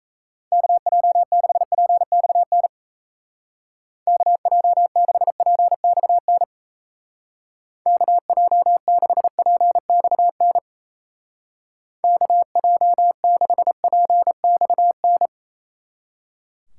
Morse code, heard as a keyed tone, about 36 wpm, 700 Hz. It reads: KJ6PXN KJ6PXN KJ6PXN KJ6PXN